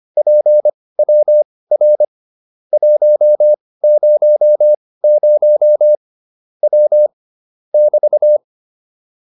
Morse code, heard as a keyed tone, about 25 wpm, spaced as for 18 wpm, 600 Hz. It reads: PWR 100 W <BT>
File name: QSO410 18